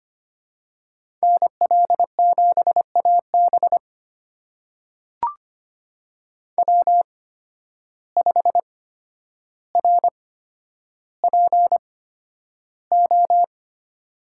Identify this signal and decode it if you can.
Morse code, heard as a keyed tone, about 25 wpm, spaced as for 11 wpm, 700 Hz. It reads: NL7AB W5RPO
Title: USA Call Signs - 5 Characters - 8x Character Spacing - 25wpm